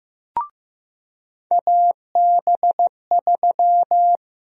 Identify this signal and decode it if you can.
Morse code, heard as a keyed tone, about 15 wpm, 700 Hz. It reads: AB3
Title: USA Call Sign Prefixes - Repeated 2x - 15wpm